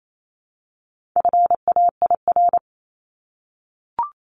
Morse code, heard as a keyed tone, about 28 wpm, 700 Hz. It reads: FAIR
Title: LY Suffix - Top 100 Words with Base Words 28wpm